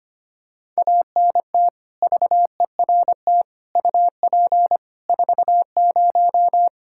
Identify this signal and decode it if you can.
Morse code, heard as a keyed tone, about 25 wpm, 700 Hz. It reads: ANT VERT UP 40
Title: Practice QSO - Exchange 2 - Info - 25wpm